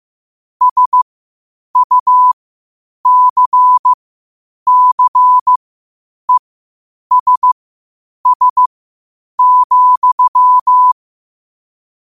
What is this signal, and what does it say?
Morse code, heard as a keyed tone, about 15 wpm, spaced as for 8 wpm, 1000 Hz. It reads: SUCCESS,
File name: qotdfiles / 8wpm